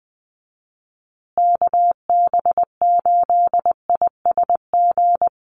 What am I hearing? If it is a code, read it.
Morse code, heard as a keyed tone, about 20 wpm, 700 Hz. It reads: KB8ISG
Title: USA Call Signs Repeated 2x 20wpm